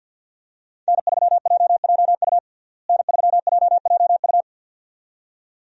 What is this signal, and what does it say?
Morse code, heard as a keyed tone, about 48 wpm, 700 Hz. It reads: N3JJU N3JJU
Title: USA Call Signs - 5 Characters - Repeated 2x - 50wpm